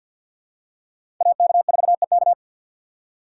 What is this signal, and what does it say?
Morse code, heard as a keyed tone, about 47 wpm, 700 Hz. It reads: AK4EK
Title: USA Call Signs - 5 Characters - 6x Character Spacing - 50wpm